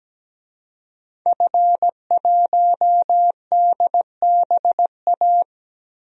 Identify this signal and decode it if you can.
Morse code, heard as a keyed tone, about 17 wpm, 700 Hz. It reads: F1DBA
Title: DXCC-100 Call Signs - 5 Characters - Repeated 3x - 17wpm